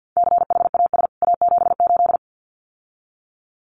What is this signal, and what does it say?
Morse code, heard as a keyed tone, about 50 wpm, 700 Hz. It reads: CHAS A78
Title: CWT Exchange 50wpm